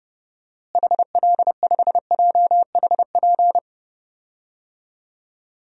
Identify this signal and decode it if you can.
Morse code, heard as a keyed tone, about 32 wpm, 700 Hz. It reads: HL5JHP
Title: DXCC-100 Call Signs - 6 Characters - Repeated 2x - 30wpm